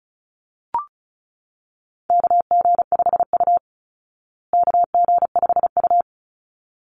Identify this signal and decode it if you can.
Morse code, heard as a keyed tone, about 34 wpm, 700 Hz. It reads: KG5U KG5U
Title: Call Signs Repeated 3x - 35wpm